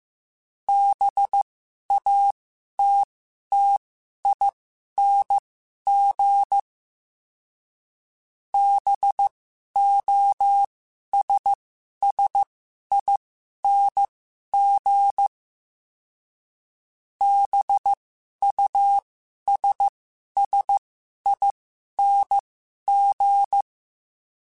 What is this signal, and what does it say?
Morse code, heard as a keyed tone, about 15 wpm, spaced as for 11 wpm, 800 Hz. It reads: BATTING BOSSING BUSSING
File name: Special INGs-mp3s / Special INGs-mp3s / Sngl Ing @10